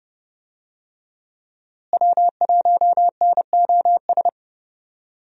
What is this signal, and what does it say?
Morse code, heard as a keyed tone, about 31 wpm, 700 Hz. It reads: W1NOS
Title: USA Call Signs - 5 Characters - Repeated 2x - 30wpm